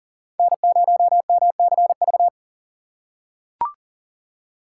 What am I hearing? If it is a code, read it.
Morse code, heard as a keyed tone, about 40 wpm, 700 Hz. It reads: N0MCV
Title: USA Call Signs - 5 Characters - 2x Character Spacing - 40wpm